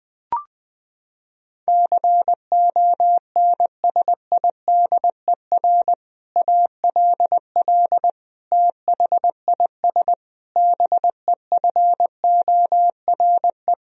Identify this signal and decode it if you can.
Morse code, heard as a keyed tone, about 20 wpm, 700 Hz. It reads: CONSIDER ALL THIS BEFORE
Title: Sentences from Top 1000 Words - Review Part 4 - 20wpm